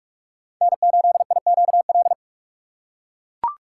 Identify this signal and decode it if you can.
Morse code, heard as a keyed tone, about 43 wpm, 700 Hz. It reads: N8IQL
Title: USA Call Signs - 5 Characters - 4x Character Spacing - 45wpm